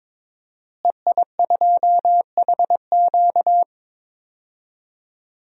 Morse code, heard as a keyed tone, about 22 wpm, 700 Hz. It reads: EI2HQ
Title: DXCC-100 Call Signs - 5 Characters - Repeated 3x - 22wpm